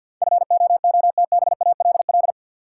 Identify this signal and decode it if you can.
Morse code, heard as a keyed tone, about 52 wpm, 700 Hz. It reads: FOOTBALL
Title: Sentences from Top 1600 Words - New Word Drill - 50wpm